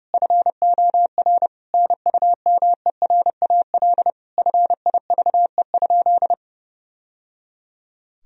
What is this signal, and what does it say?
Morse code, heard as a keyed tone, about 30 wpm, 700 Hz. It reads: FOR NUMERAL FIVE?
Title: Sentences from Top 1000 Words - Review Part 2 - 30wpm